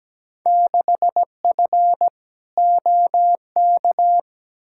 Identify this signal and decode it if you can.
Morse code, heard as a keyed tone, about 17 wpm, 700 Hz. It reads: 6F OK
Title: Field Day Exchanges Encore - 17wpm